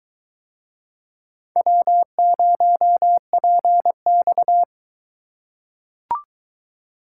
Morse code, heard as a keyed tone, about 21 wpm, 700 Hz. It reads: W0PX